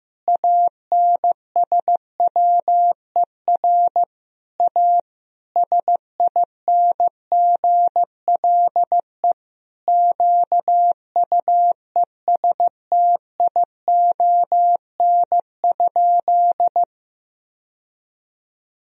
Morse code, heard as a keyed tone, about 15 wpm, 700 Hz. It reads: ANSWER A SINGLE QUESTION?
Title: Sentences from Top 1000 Words - Review Part 4 - 15wpm